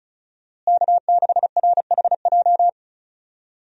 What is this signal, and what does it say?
Morse code, heard as a keyed tone, about 34 wpm, 700 Hz. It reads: K6RHJ
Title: USA Call Signs - 5 Characters - Repeated 2x - 35wpm